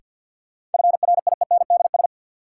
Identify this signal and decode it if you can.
Morse code, heard as a keyed tone, about 53 wpm, 700 Hz. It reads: FRIENDS
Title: Sentences from Top 1600 Words - New Word Drill Alt - 50wpm